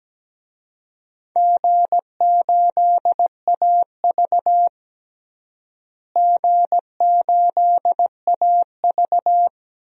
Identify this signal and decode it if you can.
Morse code, heard as a keyed tone, about 17 wpm, 700 Hz. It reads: G8AV G8AV
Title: DXCC-100 Call Signs - 4 Characters - Repeated 3x - 17wpm